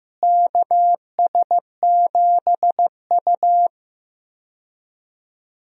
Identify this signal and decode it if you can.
Morse code, heard as a keyed tone, about 15 wpm, 700 Hz. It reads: KS7U
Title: USA Call Signs - 4 Characters - Repeated 3x - 15wpm